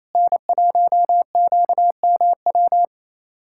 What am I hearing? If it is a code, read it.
Morse code, heard as a keyed tone, about 27 wpm, 700 Hz. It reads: N1QMW
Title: USA Callsign DE Callsign 28wpm